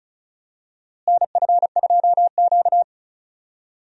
Morse code, heard as a keyed tone, about 34 wpm, 700 Hz. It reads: NF2Q
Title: USA Call Signs - 4 Characters - Repeated 3x - 35wpm